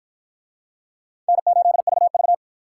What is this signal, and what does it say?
Morse code, heard as a keyed tone, about 46 wpm, 700 Hz. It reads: N8VV